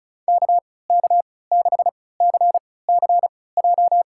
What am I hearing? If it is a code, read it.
Morse code, heard as a keyed tone, about 34 wpm, spaced as for 20 wpm, 700 Hz. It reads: KK6CCJ
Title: USA Call Signs - 6 Characters - 3x Character Spacing - 35wpm